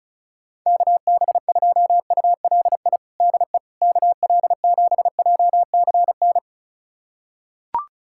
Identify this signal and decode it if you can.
Morse code, heard as a keyed tone, about 34 wpm, 700 Hz. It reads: KB2ULI DE KL7JCN
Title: USA Callsign DE Callsign 35wpm